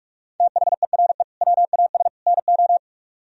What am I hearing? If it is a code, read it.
Morse code, heard as a keyed tone, about 44 wpm, 700 Hz. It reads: THERE WAS NO